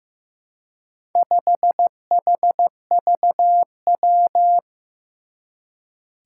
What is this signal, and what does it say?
Morse code, heard as a keyed tone, about 15 wpm, 700 Hz. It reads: 5HVW
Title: Four Challenging Letter-Number Combinations 15wpm